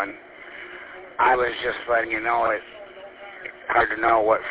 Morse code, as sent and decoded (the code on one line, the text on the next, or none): none